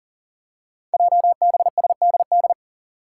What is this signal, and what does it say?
Morse code, heard as a keyed tone, about 38 wpm, 690 Hz. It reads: JBSDD
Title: Five Challenging Letter-Number Combinations 40wpm